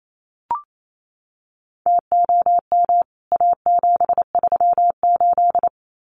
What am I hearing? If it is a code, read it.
Morse code, heard as a keyed tone, about 28 wpm, 700 Hz. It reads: TOM A738
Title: CWT Exchange 28wpm